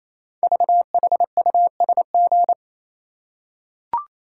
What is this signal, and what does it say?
Morse code, heard as a keyed tone, about 29 wpm, 700 Hz. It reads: VHUSG
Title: Code Groups - 5 Letters - 28wpm